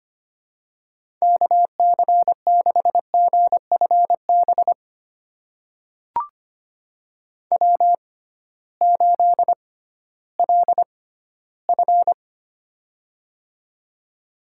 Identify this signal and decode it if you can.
Morse code, heard as a keyed tone, about 25 wpm, spaced as for 14 wpm, 700 Hz. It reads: KC6GFB W8LF
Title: USA Call Signs - 6x Character Spacing - 25wpm